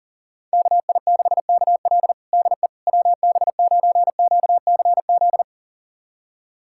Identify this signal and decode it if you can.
Morse code, heard as a keyed tone, about 40 wpm, 700 Hz. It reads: KI6KL DE WB9QCZ
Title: USA Callsign DE Callsign 40wpm